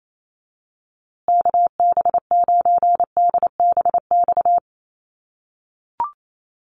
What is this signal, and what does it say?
Morse code, heard as a keyed tone, about 28 wpm, 700 Hz. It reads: KB9DBX
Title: USA Call Signs Repeated 3x 28wpm